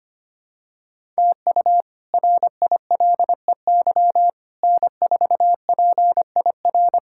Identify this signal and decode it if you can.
Morse code, heard as a keyed tone, about 25 wpm, 700 Hz. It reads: TU RILEY N4PIR
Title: K1USN SST QSO Practice - 25wpm